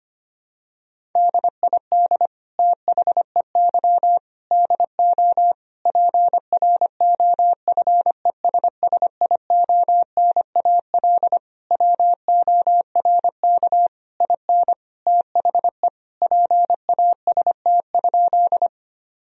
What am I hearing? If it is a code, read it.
Morse code, heard as a keyed tone, about 25 wpm, 700 Hz. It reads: DID THEY DO PROFESSIONAL WORK IN THE PAST?
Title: Sentences from Top 1000 Words - Review Part 4 - 25wpm